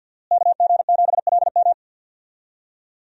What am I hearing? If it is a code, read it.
Morse code, heard as a keyed tone, about 51 wpm, 700 Hz. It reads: KG7LM
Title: USA Call Signs - 5 Characters - Repeated 2x - 50wpm